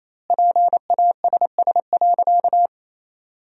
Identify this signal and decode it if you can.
Morse code, heard as a keyed tone, about 28 wpm, 700 Hz. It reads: PASS.
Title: 2x Word-Spacing - Sentences from Top 400 Words - 28wpm